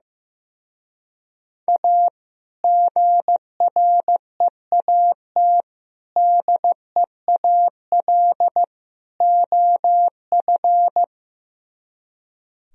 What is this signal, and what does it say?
Morse code, heard as a keyed tone, about 15 wpm, 700 Hz. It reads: A GREAT DEAL OF
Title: Speed-Racing - Sets of 4 Words 15wpm